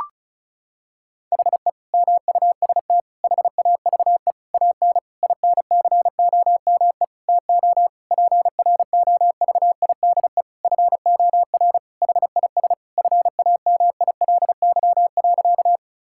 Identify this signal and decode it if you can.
Morse code, heard as a keyed tone, about 36 wpm, 700 Hz. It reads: HE MUST HAVE AN INCOME TO PROVIDE FOR HIS FAMILY.
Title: Sentences from Top 1000 Words - Review Part 5 - 35wpm